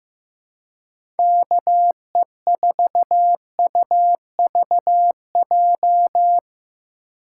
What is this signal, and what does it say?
Morse code, heard as a keyed tone, about 15 wpm, 700 Hz. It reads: KE4UVJ
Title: USA Call Signs - 6 Characters - 4x Character Spacing - 15wpm